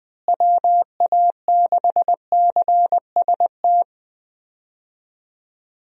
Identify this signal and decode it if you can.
Morse code, heard as a keyed tone, about 20 wpm, 700 Hz. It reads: WA6CST
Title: USA Call Signs - 6 Characters - Repeated 3x - 20wpm